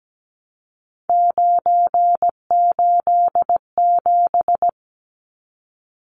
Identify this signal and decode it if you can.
Morse code, heard as a keyed tone, about 17 wpm, 700 Hz. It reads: 987
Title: Numbers - 3-Digit 17wpm